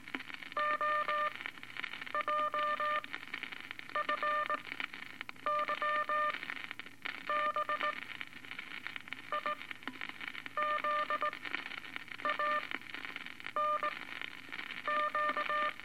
Morse code, heard as a keyed tone, about 18 wpm, spaced as for 7 wpm, 1200 Hz, with radio noise. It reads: OJFYB IZANQ
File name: Eight Words Per Minute